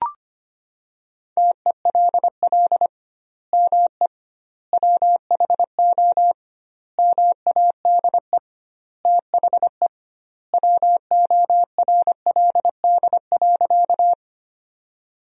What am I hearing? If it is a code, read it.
Morse code, heard as a keyed tone, about 25 wpm, 700 Hz. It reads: TELL ME WHO MADE THE WORLD.
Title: 2x Word-Spacing - Sentences from Top 200 Words - 25wpm